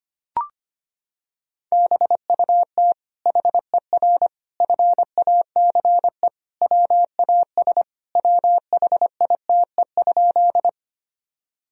Morse code, heard as a keyed tone, about 25 wpm, 700 Hz. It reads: BUT HER FACE WAS WHITE?
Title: Sentences from Top 1000 Words - Review Part 2 - 25wpm